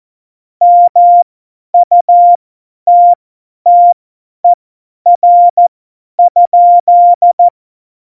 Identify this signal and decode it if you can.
Morse code, heard as a keyed tone, about 14 wpm, spaced as for 10 wpm, 700 Hz. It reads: MUTTER?
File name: Morsekurs / Witze / Witz 03. November 2020, 50 F70